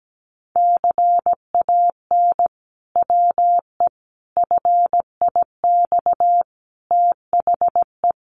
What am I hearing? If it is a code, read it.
Morse code, heard as a keyed tone, about 17 wpm, 700 Hz. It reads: CAN WE FIX THE
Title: Sentences from Top 1400 Words 17wpm